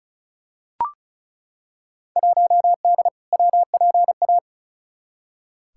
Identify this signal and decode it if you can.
Morse code, heard as a keyed tone, about 34 wpm, 700 Hz. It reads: 1D WPA
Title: Field Day Exchanges - 35wpm